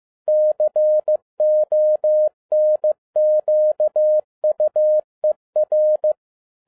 Morse code, heard as a keyed tone, about 15 wpm, 610 Hz. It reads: CONQUER
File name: 2011-07-mp3 / n37 THE 3.11 2 15wpm